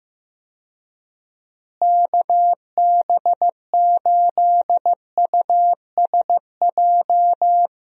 Morse code, heard as a keyed tone, about 15 wpm, 700 Hz. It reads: KB8USJ
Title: USA Call Signs - 6 Characters - Repeated 3x - 15wpm